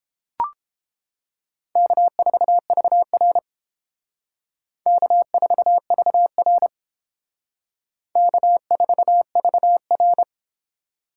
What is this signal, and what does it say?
Morse code, heard as a keyed tone, about 28 wpm, 700 Hz. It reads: K4VR K4VR K4VR